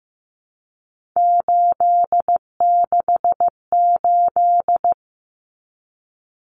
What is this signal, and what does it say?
Morse code, heard as a keyed tone, about 15 wpm, 700 Hz. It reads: 868
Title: Numbers - 3-Digit 15wpm